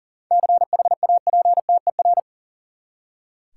Morse code, heard as a keyed tone, about 40 wpm, 700 Hz. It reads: CHAPTER
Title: Seven Letter Words - 40wpm